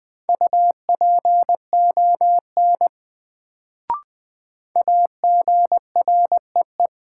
Morse code, heard as a keyed tone, about 20 wpm, 700 Hz. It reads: UPON AGREE
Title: Sentences from Top 1100 Words - New Word Drill Alt - 20wpm